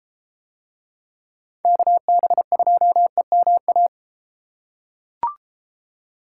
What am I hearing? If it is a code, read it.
Morse code, heard as a keyed tone, about 32 wpm, 700 Hz. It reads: KB2EMA